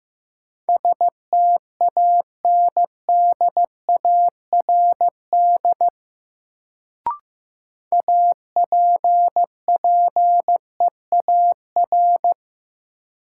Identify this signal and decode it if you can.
Morse code, heard as a keyed tone, about 15 wpm, 700 Hz. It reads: STANDARD APPEAR
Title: Sentences from Top 1200 Words - New Word Drill Alt - 15wpm